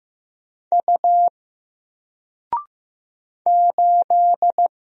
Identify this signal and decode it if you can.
Morse code, heard as a keyed tone, about 15 wpm, 700 Hz. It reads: U 8
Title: Single Character (Letter:Number:Punctuation:Prosign) 15wpm